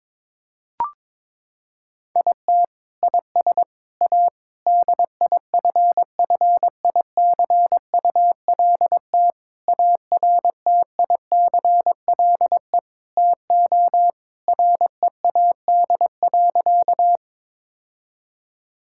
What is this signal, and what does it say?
Morse code, heard as a keyed tone, about 21 wpm, 700 Hz. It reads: IT IS A DIFFICULT ARTICLE TO READ.
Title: Sentences from Top 1000 Words - Review Part 5 - 22wpm